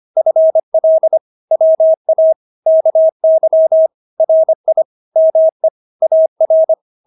Morse code, heard as a keyed tone, about 25 wpm, 640 Hz. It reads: FL WA KY RI ME AR
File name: State-Abbreviations-25